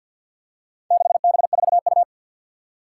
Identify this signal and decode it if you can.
Morse code, heard as a keyed tone, about 46 wpm, 700 Hz. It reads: 6B4U
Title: Four Challenging Letter-Number Combinations 50wpm